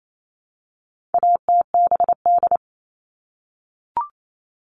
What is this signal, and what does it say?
Morse code, heard as a keyed tone, about 28 wpm, 700 Hz. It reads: ATBD
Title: All Cut Numbers - 4-Digit 28wpm